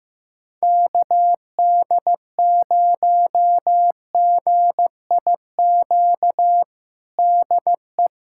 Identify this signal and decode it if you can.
Morse code, heard as a keyed tone, about 15 wpm, 700 Hz. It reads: KD0GIQ DE
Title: USA Callsign DE Callsign - 6 Characters - 15wpm